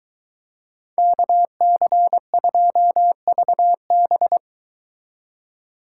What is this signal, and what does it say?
Morse code, heard as a keyed tone, about 23 wpm, 700 Hz. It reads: KC2VB